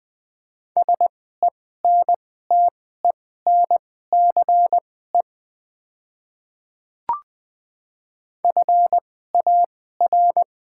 Morse code, heard as a keyed tone, about 20 wpm, spaced as for 14 wpm, 700 Hz. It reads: SENTENCE FAR
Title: 2x Character Spacing - Top 500 Words - 20wpm